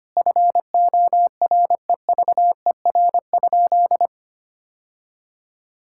Morse code, heard as a keyed tone, about 25 wpm, 700 Hz. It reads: FOREVER?
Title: Sentences from Top 1600 Words 25wpm